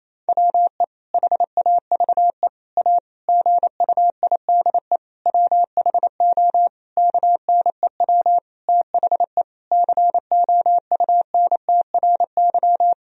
Morse code, heard as a keyed tone, about 28 wpm, 700 Hz. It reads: WE HAVE A GUIDE WHO KNEW THE COUNTRY
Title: Sentences from Top 1000 Words - Review Part 3 - 28wpm